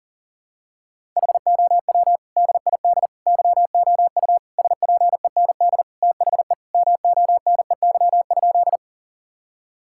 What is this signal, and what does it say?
Morse code, heard as a keyed tone, about 40 wpm, 700 Hz. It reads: HOW DID YOU SPEND THE MONEY?
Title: Sentences from Top 1000 Words - Review Part 5 - 40wpm